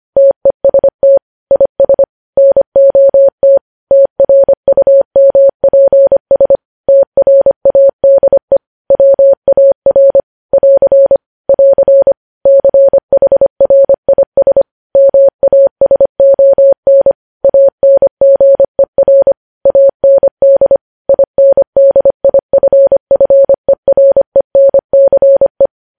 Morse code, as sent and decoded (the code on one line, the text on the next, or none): - . ... - / .. ... / -. --- - / - .-. ..- -- .--. ... / - .-. .- -.. / .-- .- .-. / .-.-. / .-.-. / -.-. .... .-. .. ... / -- .- ... --- -. / .- -. --. .-. / .- -. -.. / .. -. -.. .. ..-. ..-. .-. -. -.-. .
TEST IS NOT TRUMPS TRADE WAR <AR> <AR> CHRIS MASON ANGER AND INDIFFERENCE